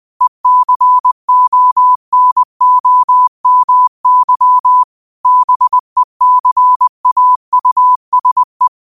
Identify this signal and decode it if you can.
Morse code, heard as a keyed tone, about 20 wpm, 1000 Hz. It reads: ECONOMY BECAUSE
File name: qotdfiles / 20wpm